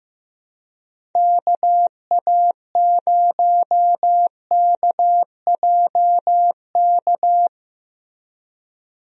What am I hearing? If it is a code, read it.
Morse code, heard as a keyed tone, about 15 wpm, 700 Hz. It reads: KA0KJK
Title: USA Call Signs - 6 Characters - 15wpm